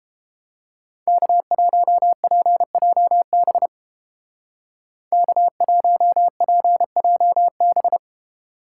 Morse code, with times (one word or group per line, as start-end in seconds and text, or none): 1.1-3.7 K1PJB
5.1-8.0 K1PJB